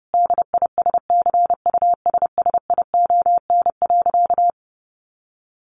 Morse code, heard as a keyed tone, about 30 wpm, 700 Hz. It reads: DISCUSSION.
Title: Sentences from Top 1100 Words 30wpm